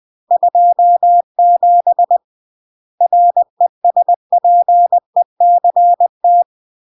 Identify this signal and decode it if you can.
Morse code, heard as a keyed tone, about 20 wpm, 700 Hz. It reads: 27 RESPECT
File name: CW / jodorowsky / 82 commandments 20-20